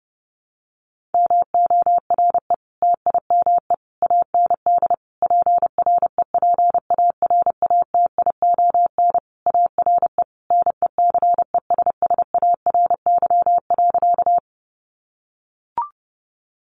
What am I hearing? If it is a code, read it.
Morse code, heard as a keyed tone, about 30 wpm, 700 Hz. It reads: MORE TIME AND PREPARATION ARE NECESSARY.
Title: Sentences from Top 1400 Words 30wpm